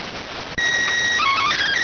SSTV (slow-scan television), PD 290.